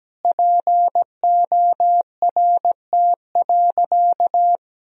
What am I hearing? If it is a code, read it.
Morse code, heard as a keyed tone, about 17 wpm, 700 Hz. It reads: PORT.
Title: Sentences from Top 1000 Words - Review Part 1 - 17wpm